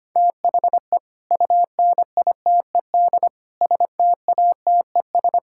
Morse code, heard as a keyed tone, about 25 wpm, 700 Hz. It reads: THE UNITED STATES
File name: practice / 25wpm / Sentences with 3-Word Phrases - 25wpm